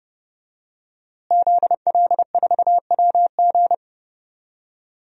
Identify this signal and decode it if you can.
Morse code, heard as a keyed tone, about 30 wpm, 700 Hz. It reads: ZL4WG
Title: DXCC-100 Call Signs - 5 Characters - Repeated 3x - 30wpm